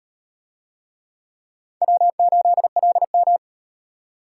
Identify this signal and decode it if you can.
Morse code, heard as a keyed tone, about 38 wpm, 700 Hz. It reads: W8LM